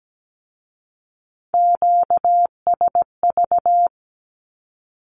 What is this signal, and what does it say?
Morse code, heard as a keyed tone, about 17 wpm, 700 Hz. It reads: QSV
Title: Q-Codes with Spoken Meaning - 17wpm